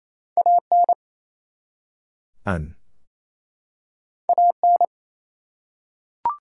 Morse code, heard as a keyed tone, about 27 wpm, 700 Hz. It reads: AN AN
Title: Two Letter Words - 28wpm